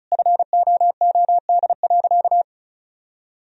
Morse code, heard as a keyed tone, about 36 wpm, 700 Hz. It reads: FOOD.
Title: Sentences from Top 1000 Words - Review Part 1 - 35wpm